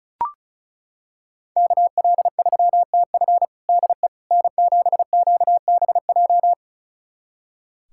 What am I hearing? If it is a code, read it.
Morse code, heard as a keyed tone, about 34 wpm, 690 Hz. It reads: KL3TF DE N7QBJ
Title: USA Callsign DE Callsign - 5 Characters - 35wpm